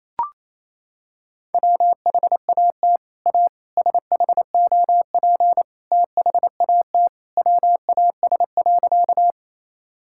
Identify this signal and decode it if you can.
Morse code, heard as a keyed tone, about 28 wpm, 700 Hz. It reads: WHAT A SHOP THAT WAS.
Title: Sentences from Top 1600 Words 28wpm